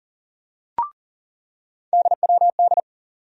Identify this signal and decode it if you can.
Morse code, heard as a keyed tone, about 40 wpm, 700 Hz. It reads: DWD
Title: USA Call Sign Postfixes - 3 Letters 40wpm